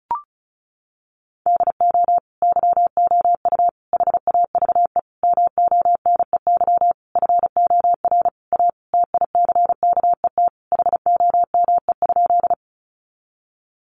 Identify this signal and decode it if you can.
Morse code, heard as a keyed tone, about 34 wpm, 700 Hz. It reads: DO YOU HAVE MONEY FOR A TICKET HOME?
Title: Sentences from Top 1400 Words 35wpm